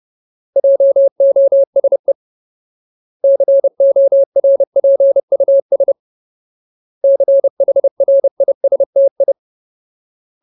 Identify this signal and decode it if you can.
Morse code, heard as a keyed tone, about 30 wpm, 550 Hz. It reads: JOSE CORPUS CHRISTI
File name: Ultimate Word List / 30WPM Ultimate